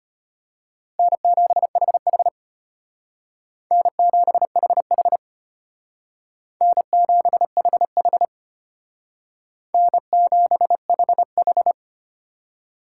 Morse code, heard as a keyed tone, about 33 wpm, 700 Hz. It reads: N7HH N7HH N7HH N7HH